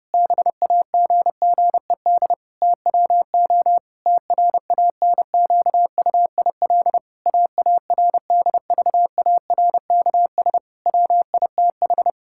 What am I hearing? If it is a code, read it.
Morse code, heard as a keyed tone, about 29 wpm, 700 Hz. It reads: BAGGED TWO TRANQUIL AARDVARKS WITH
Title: Pangram Challenge v2 - 30wpm